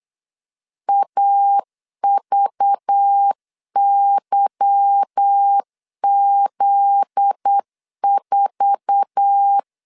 Morse code, heard as a keyed tone, about 8 wpm, 800 Hz. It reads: AVYZ4
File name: LEZIONI / 9 2867 8wpm 3440